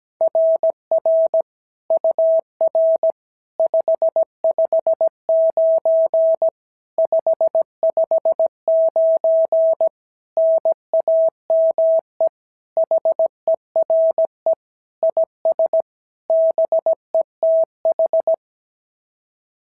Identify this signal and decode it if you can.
Morse code, heard as a keyed tone, about 16 wpm, 650 Hz. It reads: RR UR 559 559 NAME HERE IS BETH
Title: SOTA QSO Practice 17wpm